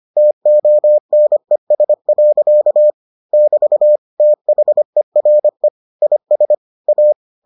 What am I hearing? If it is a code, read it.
Morse code, heard as a keyed tone, about 25 wpm, 600 Hz. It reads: TONES. <BT> THERE IS A